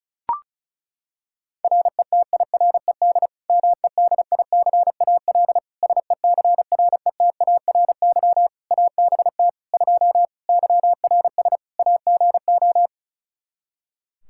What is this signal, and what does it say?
Morse code, heard as a keyed tone, about 36 wpm, 700 Hz. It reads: RETIRED MEDICAL SECRETARY ABT 2 YRS AGO
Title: Practice QSO - Exchange 3 - More Info - 35wpm